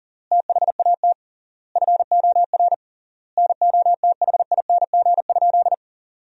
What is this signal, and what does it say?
Morse code, heard as a keyed tone, about 40 wpm, 700 Hz. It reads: THAT FOR NOTHING?
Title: 3x Word-Spacing - Sentences from Top 500 Words - 40wpm